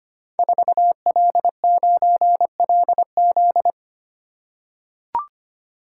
Morse code, heard as a keyed tone, about 25 wpm, 700 Hz. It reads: 4L9LZ
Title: Code Groups - 5 Letter:Numbers - 25wpm